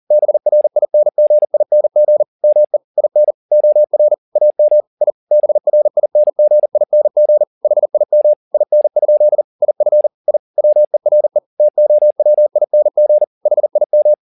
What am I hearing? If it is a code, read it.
Morse code, heard as a keyed tone, about 40 wpm, 600 Hz. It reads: BRINGING ME IN OR AM I BRINGING HIM IN? IF I WERE TOWING HIM
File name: the old man and the sea 2 40wpm